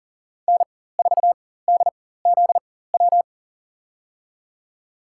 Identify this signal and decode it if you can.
Morse code, heard as a keyed tone, about 40 wpm, spaced as for 19 wpm, 700 Hz. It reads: N4DZW
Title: USA Call Signs - 5 Characters - 4x Character Spacing - 40wpm